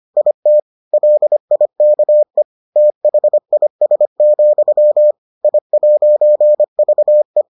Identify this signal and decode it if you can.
Morse code, heard as a keyed tone, about 25 wpm, 600 Hz. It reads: IT LIKE THIS, I'VE